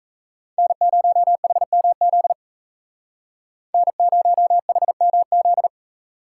Morse code, heard as a keyed tone, about 33 wpm, 700 Hz. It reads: N0HMZ N0HMZ
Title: Speed Racing - USA 1x3 Call Signs 28wpm